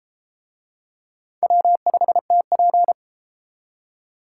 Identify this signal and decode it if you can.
Morse code, heard as a keyed tone, about 30 wpm, 700 Hz. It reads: W5TP